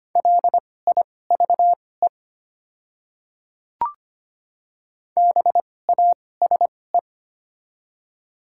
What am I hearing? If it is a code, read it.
Morse code, heard as a keyed tone, about 25 wpm, spaced as for 18 wpm, 700 Hz. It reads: LIVE BASE
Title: 2x Character Spacing - Top 300 Words - 25wpm